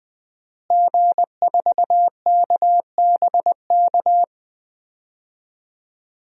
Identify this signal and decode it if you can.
Morse code, heard as a keyed tone, about 20 wpm, 700 Hz. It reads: G4KBK
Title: DXCC-100 Call Signs - 5 Characters - Repeated 2x - 20wpm